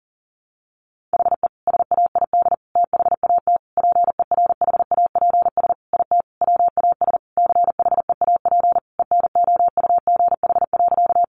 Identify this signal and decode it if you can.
Morse code, heard as a keyed tone, about 40 wpm, 700 Hz. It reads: HE SAID THAT PERHAPS IT WAS CHEAP ENOUGH.
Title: Sentences from Top 1200 Words 40wpm